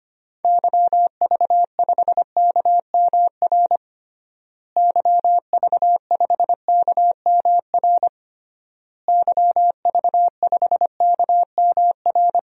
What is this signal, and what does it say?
Morse code, heard as a keyed tone, about 25 wpm, 700 Hz. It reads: YV5KMR YV5KMR YV5KMR
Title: DXCC-100 Call Signs - 6 Characters - Repeated 3x - 25wpm